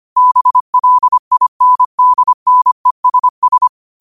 Morse code, heard as a keyed tone, about 25 wpm, 1000 Hz. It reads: BLINDNESS
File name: 25wpm